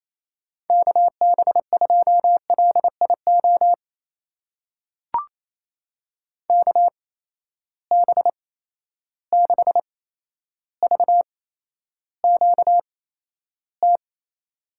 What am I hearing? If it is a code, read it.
Morse code, heard as a keyed tone, about 28 wpm, spaced as for 12 wpm, 700 Hz. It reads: KB2LIO KB6VQT
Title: USA Call Signs - 6 Characters - 8x Character Spacing - 28wpm